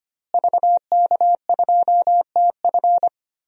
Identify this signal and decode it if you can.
Morse code, heard as a keyed tone, about 25 wpm, 700 Hz. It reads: VK2TF
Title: DXCC-100 Call Signs - 5 Characters - Repeated 3x - 25wpm